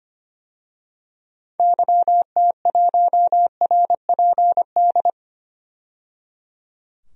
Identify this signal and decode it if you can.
Morse code, heard as a keyed tone, about 25 wpm, 700 Hz. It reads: YT1RPD
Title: DXCC-100 Call Signs - 6 Characters - 25wpm